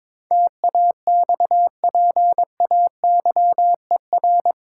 Morse code, heard as a keyed tone, about 21 wpm, 700 Hz. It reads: TAXPAYER
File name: practice / 22wpm / Eight Letter Words - 22wpm